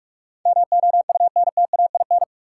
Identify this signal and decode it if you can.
Morse code, heard as a keyed tone, about 46 wpm, 700 Hz. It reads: MOUNTAIN
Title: Sentences from Top 1300 Words - New Word Drill Alt - 45wpm